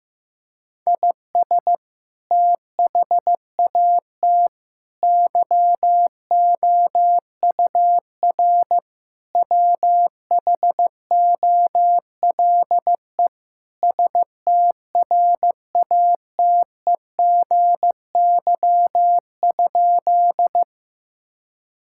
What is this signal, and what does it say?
Morse code, heard as a keyed tone, about 15 wpm, 700 Hz. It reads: IS THAT YOUR WHOLE STRATEGY?
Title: Sentences from Top 1000 Words - Review Part 5 - 15wpm